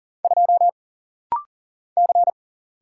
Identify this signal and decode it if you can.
Morse code, heard as a keyed tone, about 42 wpm, 700 Hz. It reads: 2 C
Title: Single Letter-Number Alt - Rapid-Fire 40wpm